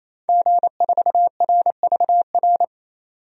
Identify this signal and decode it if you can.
Morse code, heard as a keyed tone, about 28 wpm, 700 Hz. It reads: G4RVR
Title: DXCC-100 Call Signs - 5 Characters - 28wpm